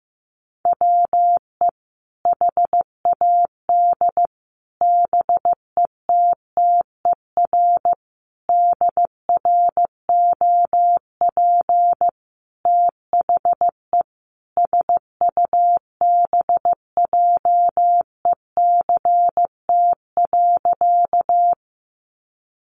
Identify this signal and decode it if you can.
Morse code, heard as a keyed tone, about 15 wpm, 700 Hz. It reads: WE HAD BETTER DROP THE SUBJECT.
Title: Sentences from Top 1400 Words 15wpm